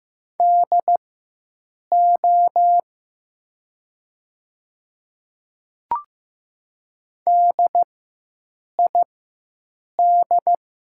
Morse code, heard as a keyed tone, about 15 wpm, spaced as for 7 wpm, 700 Hz. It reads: DO DID
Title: 4x Character Spacing - Top 100 Words - 15wpm